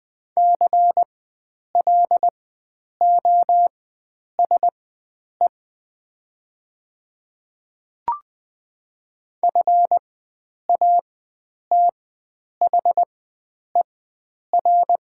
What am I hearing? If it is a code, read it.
Morse code, heard as a keyed tone, about 20 wpm, spaced as for 9 wpm, 700 Hz. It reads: CLOSE FATHER
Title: 4x Character Spacing - Top 300 Words - 20wpm